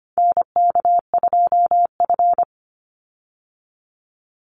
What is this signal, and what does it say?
Morse code, heard as a keyed tone, about 24 wpm, 700 Hz. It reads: NK2F
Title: Call Signs Repeated 3x - 25wpm